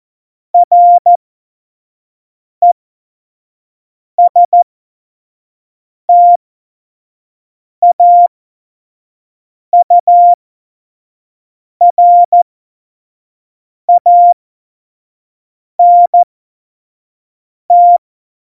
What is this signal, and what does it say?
Morse code, heard as a keyed tone, about 14 wpm, spaced as for 5 wpm, 700 Hz. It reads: RESTAURANT
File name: Morsekurs / Witze / Witz 14. Januar 2020, 25 F70